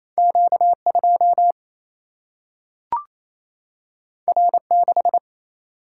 Morse code, heard as a keyed tone, about 28 wpm, 700 Hz. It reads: Q2 R6
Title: Single-Letter Single-Number 28wpm